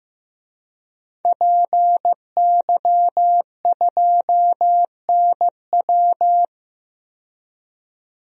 Morse code, heard as a keyed tone, about 15 wpm, 700 Hz. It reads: PY2NW
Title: DXCC-100 Call Signs - 5 Characters - 15wpm